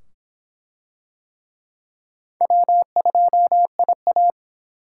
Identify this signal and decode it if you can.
Morse code, heard as a keyed tone, about 25 wpm, 700 Hz. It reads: W2IA